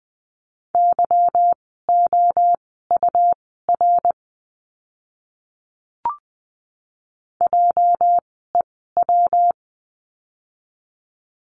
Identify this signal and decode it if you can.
Morse code, heard as a keyed tone, about 20 wpm, spaced as for 15 wpm, 700 Hz. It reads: YOUR JEW
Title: 206 Lesson 33f - Letter j - Words 20wpm